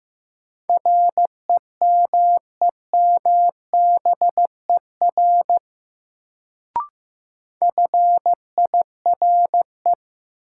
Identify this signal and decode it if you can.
Morse code, heard as a keyed tone, about 15 wpm, 700 Hz. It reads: REMEMBER FIRE
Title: Sentences from Top 400 Words - New Word Drill - 15wpm